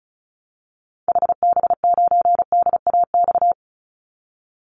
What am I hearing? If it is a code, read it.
Morse code, heard as a keyed tone, about 34 wpm, 700 Hz. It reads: HB9DAX
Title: Call Signs Repeated 2x - 35wpm